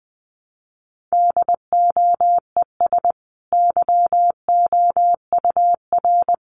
Morse code, heard as a keyed tone, about 20 wpm, 700 Hz. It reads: DOES YOUR
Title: Sentences from Top 1200 Words 20wpm